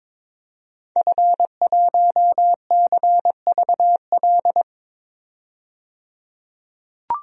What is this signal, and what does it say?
Morse code, heard as a keyed tone, about 22 wpm, 700 Hz. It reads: F1CVL
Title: DXCC-100 Call Signs - 5 Characters - 22wpm